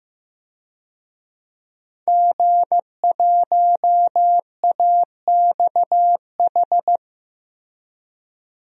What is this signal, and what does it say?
Morse code, heard as a keyed tone, about 15 wpm, 700 Hz. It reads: G1AXH
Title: DXCC-100 Call Signs - 5 Characters - Repeated 2x - 15wpm